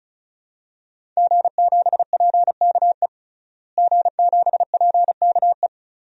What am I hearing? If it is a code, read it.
Morse code, heard as a keyed tone, about 34 wpm, 700 Hz. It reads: G7PKE G7PKE
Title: DXCC-100 Call Signs - 5 Characters - Repeated 3x - 35wpm